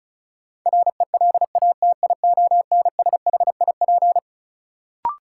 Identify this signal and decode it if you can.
Morse code, heard as a keyed tone, about 34 wpm, 700 Hz. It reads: RELATIONSHIP
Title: Sentences from Top 1100 Words - New Word Drill Alt - 35wpm